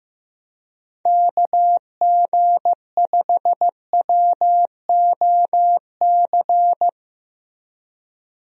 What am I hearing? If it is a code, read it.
Morse code, heard as a keyed tone, about 15 wpm, 700 Hz. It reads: KG5WOC
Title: USA Call Signs - 6 Characters - Repeated 3x - 15wpm